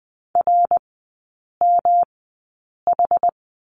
Morse code, heard as a keyed tone, about 20 wpm, spaced as for 8 wpm, 700 Hz. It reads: RMH
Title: 121 Lesson 20e - Number 3 - Three Characters 20wpm